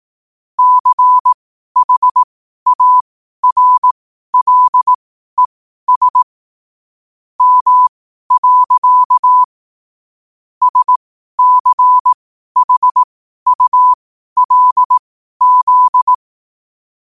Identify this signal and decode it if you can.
Morse code, heard as a keyed tone, about 18 wpm, spaced as for 12 wpm, 1000 Hz. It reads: CHARLES M. SCHULZ
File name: qotdfiles / 12wpm